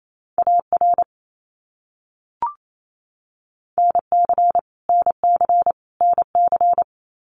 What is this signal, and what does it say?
Morse code, heard as a keyed tone, about 28 wpm, 700 Hz. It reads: AR NC NC NC
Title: US States Repeated 3x 28wpm